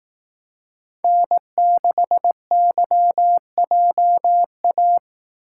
Morse code, heard as a keyed tone, about 18 wpm, 700 Hz. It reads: N6YJA
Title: Speed Racing - USA 1x3 Call Signs 15wpm